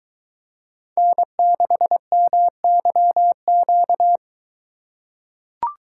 Morse code, heard as a keyed tone, about 22 wpm, 700 Hz. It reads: N6MYQ